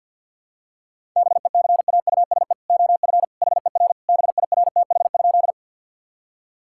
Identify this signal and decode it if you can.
Morse code, heard as a keyed tone, about 50 wpm, 700 Hz. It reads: BECAUSE OF HER BIRTH?
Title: Sentences from Top 1000 Words - Review Part 5 - 50wpm